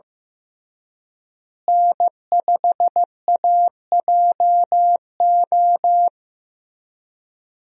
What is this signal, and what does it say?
Morse code, heard as a keyed tone, about 16 wpm, 700 Hz. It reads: N5AJO
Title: USA Call Signs - 5 Characters - Repeated 3x - 15wpm